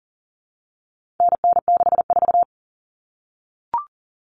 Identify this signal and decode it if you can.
Morse code, heard as a keyed tone, about 40 wpm, 700 Hz. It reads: NN64
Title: Common Cut Numbers - 4-Digit 40wpm